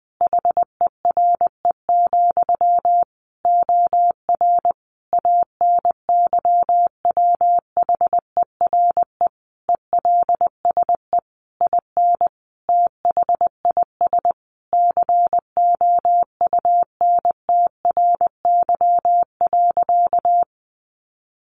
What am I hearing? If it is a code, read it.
Morse code, heard as a keyed tone, about 20 wpm, 700 Hz. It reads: HERE, OR ANYWHERE ELSE IN THIS COUNTRY.
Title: Sentences from Top 1400 Words 20wpm